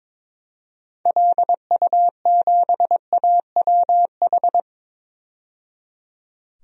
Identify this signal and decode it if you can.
Morse code, heard as a keyed tone, about 22 wpm, 700 Hz. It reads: LU7AWH
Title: DXCC-100 Call Signs - 6 Characters - 22wpm